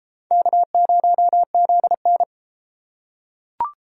Morse code, heard as a keyed tone, about 32 wpm, 700 Hz. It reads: K0ZN